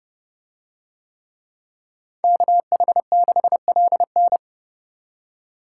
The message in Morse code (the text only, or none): KH6LN